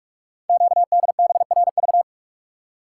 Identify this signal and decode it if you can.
Morse code, heard as a keyed tone, about 44 wpm, 700 Hz. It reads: QDBRV
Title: Code Groups - 5 Letters - 45wpm